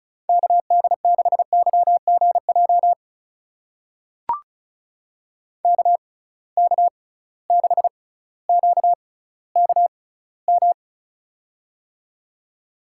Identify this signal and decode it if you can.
Morse code, heard as a keyed tone, about 36 wpm, spaced as for 18 wpm, 700 Hz. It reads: KD6YGJ KK6QKM
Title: USA Call Signs - 6 Characters - 6x Character Spacing - 35wpm